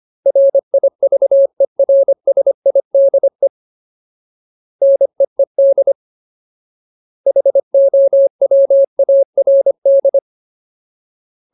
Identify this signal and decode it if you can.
Morse code, heard as a keyed tone, about 25 wpm, 550 Hz. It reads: RIVERSIDE NEED HOWARD
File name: Ultimate Word List / 25WPM Ultimate